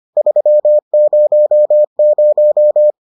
Morse code, heard as a keyed tone, about 25 wpm, 600 Hz. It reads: 300